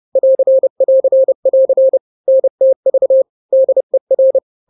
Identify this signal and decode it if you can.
Morse code, heard as a keyed tone, about 30 wpm, 530 Hz. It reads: <AR><AR><AR> NTV DER
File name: daily / 29wpm